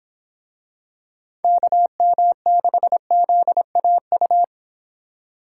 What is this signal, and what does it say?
Morse code, heard as a keyed tone, about 24 wpm, 700 Hz. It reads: KM6ZAU